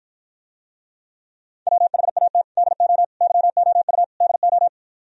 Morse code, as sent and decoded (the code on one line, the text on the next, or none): .-- .... .- - / -.. --- / -.-- --- ..- / -.. ---
WHAT DO YOU DO